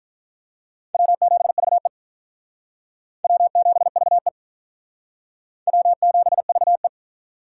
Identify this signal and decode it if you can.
Morse code, heard as a keyed tone, about 42 wpm, 700 Hz. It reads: W7VE W7VE W7VE
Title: Speed Racing - USA 1x2 Call Signs 35wpm